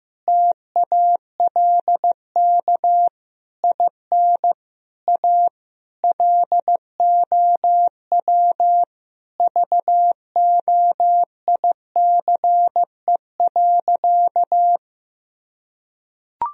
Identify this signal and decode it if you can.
Morse code, heard as a keyed tone, about 15 wpm, 700 Hz. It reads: TALK IN A LOW VOICE.
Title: Sentences from Top 1000 Words - Review Part 1 - 15wpm